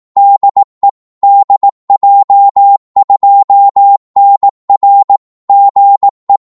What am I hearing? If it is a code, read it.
Morse code, heard as a keyed tone, about 18 wpm, 800 Hz. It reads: DE DJ2NR GE
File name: Morse / qso-18wpm